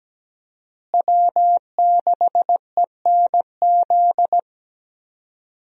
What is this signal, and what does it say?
Morse code, heard as a keyed tone, about 17 wpm, 700 Hz. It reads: W6ENZ
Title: USA Call Signs - 5 Characters - Repeated 3x - 17wpm